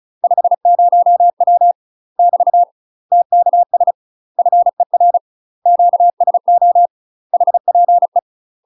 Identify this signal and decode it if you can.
Morse code, heard as a keyed tone, about 34 wpm, 700 Hz. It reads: 50W <BT> TKS FER QSO HPE
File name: UK MASTER QSOs TRAINING / 35WPM / UK MASTER QSOs 0005